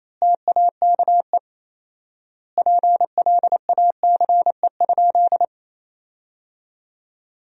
Morse code, heard as a keyed tone, about 28 wpm, 700 Hz. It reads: TAKE PLACE?
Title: 4x Word-Spacing - Sentences from Top 200 Words - 28wpm